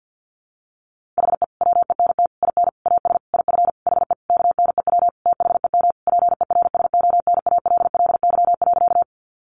Morse code, heard as a keyed tone, about 50 wpm, 700 Hz. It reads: HE WENT IN AS IF HE KNEW THEM PERSONALLY.
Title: Sentences from Top 1400 Words 50wpm